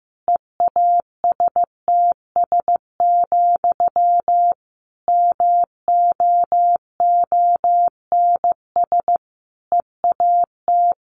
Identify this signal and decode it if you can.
Morse code, heard as a keyed tone, about 15 wpm, 700 Hz. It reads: EASTS, MOONS EAT
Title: A Sound Of Thunder 15wpm